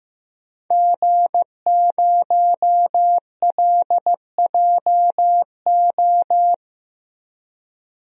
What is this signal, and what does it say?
Morse code, heard as a keyed tone, about 15 wpm, 700 Hz. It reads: G0LJO
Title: DXCC-100 Call Signs - 5 Characters - 15wpm